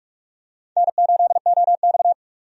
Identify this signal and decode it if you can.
Morse code, heard as a keyed tone, about 46 wpm, 700 Hz. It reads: N8OX
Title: USA Call Signs - 4 Characters - 45wpm